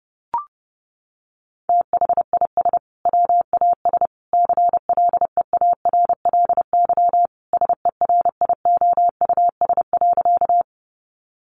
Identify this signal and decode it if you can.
Morse code, heard as a keyed tone, about 30 wpm, 700 Hz. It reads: THIS WAS CLEARLY SERIOUS.
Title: Sentences from Top 1300 Words 30wpm